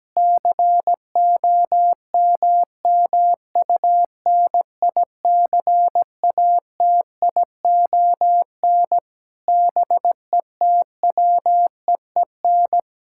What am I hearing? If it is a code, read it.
Morse code, heard as a keyed tone, about 17 wpm, 700 Hz. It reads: COMMUNICATION BETWEEN
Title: Sentences from Top 1000 Words - Review Part 6 - 17wpm